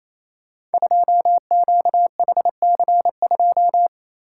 Morse code, heard as a keyed tone, about 28 wpm, 690 Hz. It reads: 2QHC2
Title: Code Groups - 5 Letter:Numbers - 28wpm